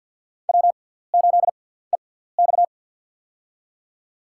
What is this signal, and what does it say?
Morse code, heard as a keyed tone, about 49 wpm, spaced as for 17 wpm, 700 Hz. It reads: W8EX
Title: USA Call Signs - 6x Character Spacing - 50wpm